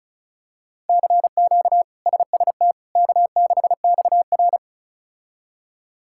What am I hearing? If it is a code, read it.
Morse code, heard as a keyed tone, about 36 wpm, 700 Hz. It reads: CQ SST K6XR
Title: K1USN SST QSO Practice - 35wpm